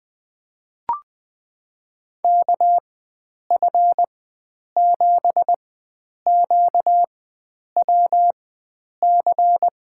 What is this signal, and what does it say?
Morse code, heard as a keyed tone, about 21 wpm, spaced as for 9 wpm, 700 Hz. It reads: KF7QWC
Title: USA Call Signs - 6 Characters - 4x Character Spacing - 20wpm